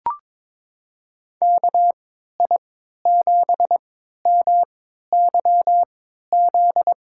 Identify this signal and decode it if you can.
Morse code, heard as a keyed tone, about 22 wpm, spaced as for 12 wpm, 700 Hz. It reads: KI7MYZ
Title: USA Call Signs - 6 Characters - 3x Character Spacing - 22wpm